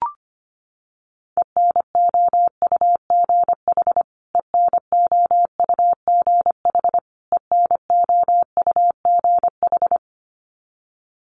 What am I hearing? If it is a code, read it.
Morse code, heard as a keyed tone, about 25 wpm, 700 Hz. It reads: ENOUGH ENOUGH ENOUGH
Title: Top 500 Words Repeated 25wpm